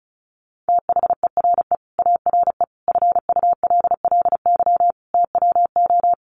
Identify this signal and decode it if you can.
Morse code, heard as a keyed tone, about 34 wpm, 700 Hz. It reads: THERE ARE FULLY TWO
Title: Sentences from Top 1300 Words 35wpm